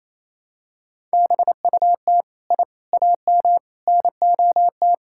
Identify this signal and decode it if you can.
Morse code, heard as a keyed tone, about 28 wpm, 700 Hz. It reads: BUT I AM NOT
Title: Sentences from Top 1000 Words - Review Part 6 - 28wpm